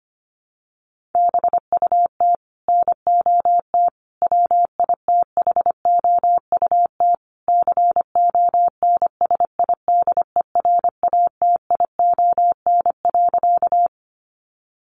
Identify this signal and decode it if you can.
Morse code, heard as a keyed tone, about 25 wpm, 700 Hz. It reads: BUT NOT WITHOUT CONSIDERATION.
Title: Sentences from Top 1500 Words 25wpm